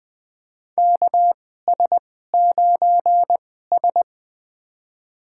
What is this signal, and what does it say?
Morse code, heard as a keyed tone, about 20 wpm, spaced as for 14 wpm, 700 Hz. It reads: KS9S
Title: USA Call Signs - 4 Characters - 2x Character Spacing - 20wpm